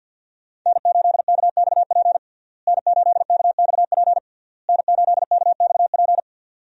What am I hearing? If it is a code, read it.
Morse code, heard as a keyed tone, about 52 wpm, 700 Hz. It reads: N8KXP N8KXP N8KXP
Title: USA Call Signs - 5 Characters - Repeated 3x - 50wpm